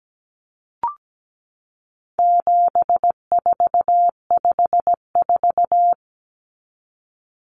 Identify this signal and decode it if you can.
Morse code, heard as a keyed tone, about 17 wpm, 700 Hz. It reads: 7454
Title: Numbers - 4-Digit 17wpm